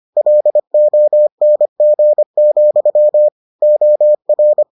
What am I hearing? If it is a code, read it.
Morse code, heard as a keyed tone, about 25 wpm, 600 Hz. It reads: LONG, OR